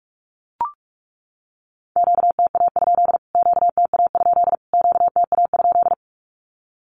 Morse code, heard as a keyed tone, about 44 wpm, 700 Hz. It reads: QTA? QTA? QTA?
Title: Q-Codes Repeated 3x 45wpm